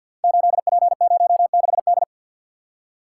Morse code, heard as a keyed tone, about 46 wpm, 700 Hz. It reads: 8P06D
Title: Code Groups - 5 Letter:Numbers - 50wpm